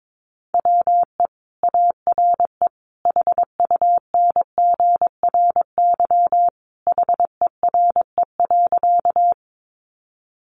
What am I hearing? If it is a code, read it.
Morse code, heard as a keyed tone, about 22 wpm, 700 Hz. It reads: WE ARE HUNGRY HERE.
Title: Sentences from Top 1400 Words 22wpm